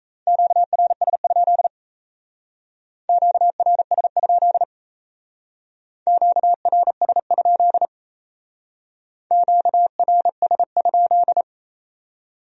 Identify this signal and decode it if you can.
Morse code, heard as a keyed tone, about 34 wpm, 700 Hz. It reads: QRS? QRS? QRS? QRS?
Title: Speed-Racing Q-Codes - 28wpm